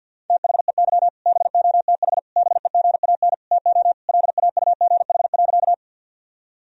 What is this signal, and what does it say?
Morse code, heard as a keyed tone, about 50 wpm, 700 Hz. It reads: THEY BOTH BEGAN TO LAUGH.
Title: Sentences from Top 1000 Words - Review Part 1 - 50wpm